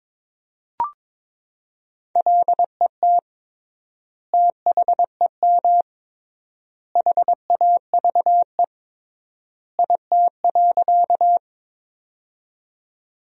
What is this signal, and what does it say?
Morse code, heard as a keyed tone, about 22 wpm, 700 Hz. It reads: LET THEM HAVE IT.
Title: 3x Word-Spacing - Sentences from Top 300 Words - 22wpm